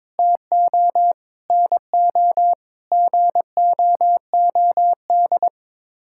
Morse code, heard as a keyed tone, about 22 wpm, 700 Hz. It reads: TO NO GOOD
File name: Sets of 9 Words - 22wpm